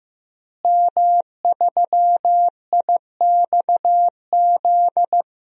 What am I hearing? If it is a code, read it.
Morse code, heard as a keyed tone, about 15 wpm, 700 Hz. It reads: M3IXZ
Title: DXCC-100 Call Signs - 5 Characters - Repeated 3x - 15wpm